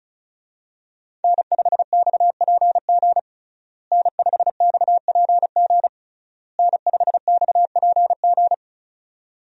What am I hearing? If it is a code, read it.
Morse code, heard as a keyed tone, about 36 wpm, 700 Hz. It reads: N5XPG N5XPG N5XPG
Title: USA Call Signs - 5 Characters - Repeated 3x - 35wpm